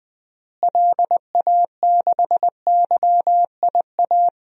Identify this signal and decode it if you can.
Morse code, heard as a keyed tone, about 20 wpm, 700 Hz. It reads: LA6YIA
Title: DXCC-100 Call Signs - 6 Characters - Repeated 3x - 20wpm